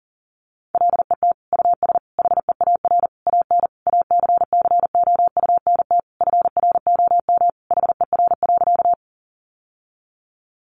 Morse code, heard as a keyed tone, about 40 wpm, 700 Hz. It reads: LET US HEAR AN ACCOUNT FROM HER.
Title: Sentences from Top 900 Words 40wpm